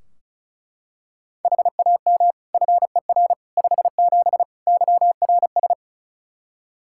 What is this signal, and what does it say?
Morse code, heard as a keyed tone, about 36 wpm, 700 Hz. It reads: HAM FER 57 YRS
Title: Practice QSO - Exchange 3 - More Info - 35wpm